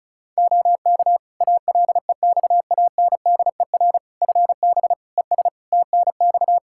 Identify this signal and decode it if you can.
Morse code, heard as a keyed tone, about 34 wpm, 700 Hz. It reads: OK ALEXANDER FB ES TNX
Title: Practice QSO - Exchange 2 - Info - 35wpm